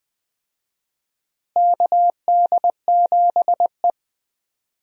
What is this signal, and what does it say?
Morse code, heard as a keyed tone, about 20 wpm, 700 Hz. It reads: KD7E
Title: USA Call Signs - 4 Characters - Repeated 2x - 20wpm